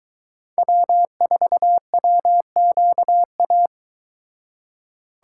Morse code, heard as a keyed tone, about 22 wpm, 700 Hz. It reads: W4WQA